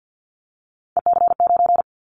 Morse code, heard as a keyed tone, about 50 wpm, 700 Hz. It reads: EC9
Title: Callsign Prefixes for Top 100 DXCC - 50wpm